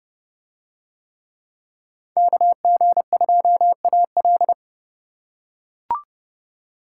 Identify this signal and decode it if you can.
Morse code, heard as a keyed tone, about 30 wpm, 700 Hz. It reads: KG2AL